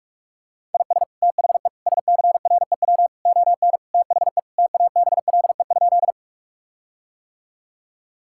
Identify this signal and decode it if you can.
Morse code, heard as a keyed tone, about 46 wpm, 700 Hz. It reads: IS THE SCREW ON THE TABLE?
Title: Sentences from Top 1600 Words 45wpm